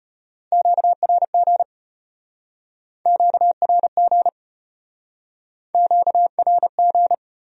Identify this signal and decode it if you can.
Morse code, heard as a keyed tone, about 36 wpm, 700 Hz. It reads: QRG QRG QRG